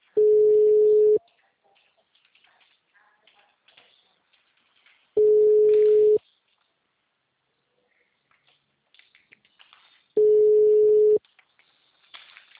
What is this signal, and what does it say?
Ringback tone (Europe, 425 Hz).
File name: out- -2002- - - .3013